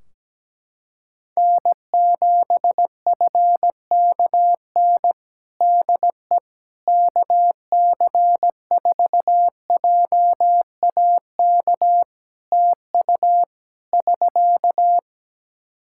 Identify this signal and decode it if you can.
Morse code, heard as a keyed tone, about 17 wpm, 700 Hz. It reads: N7FKN DE KC4JAK TU <SK>
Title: Practice QSO - Exchange 4 - Ending - 17wpm